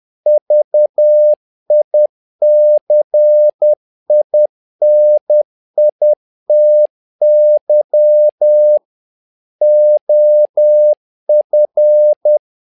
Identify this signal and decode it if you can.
Morse code, heard as a keyed tone, about 10 wpm, 600 Hz. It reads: VICINITY OF
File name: learning center / tower trouble / tower trouble 10wpm